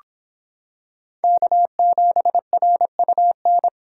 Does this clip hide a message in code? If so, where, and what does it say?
Yes: Morse code from 1.2 to 3.7 s, heard as a keyed tone, about 26 wpm, 700 Hz. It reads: K7RUN